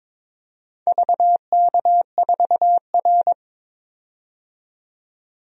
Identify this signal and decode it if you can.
Morse code, heard as a keyed tone, about 23 wpm, 700 Hz. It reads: VK4R
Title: DXCC-100 Call Signs - 4 Characters - 22wpm